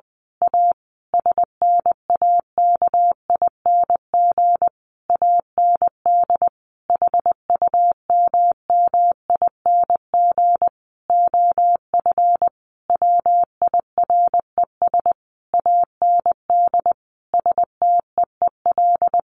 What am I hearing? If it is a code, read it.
Morse code, heard as a keyed tone, about 20 wpm, 700 Hz. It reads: A SNAKING AND HUMMING OF WIRES AND STEEL
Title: A Sound Of Thunder 20wpm